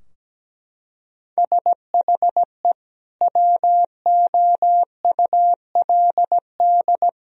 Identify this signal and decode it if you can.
Morse code, heard as a keyed tone, about 17 wpm, 700 Hz. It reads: SHE WOULD
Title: Sentences from Top 1000 Words - Review Part 3 - 17wpm